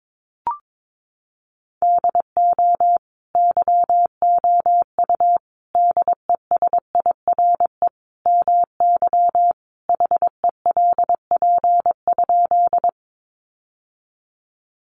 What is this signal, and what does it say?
Morse code, heard as a keyed tone, about 22 wpm, 700 Hz. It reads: DO YOU DESIRE MY HELP?
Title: Sentences from Top 1100 Words 22wpm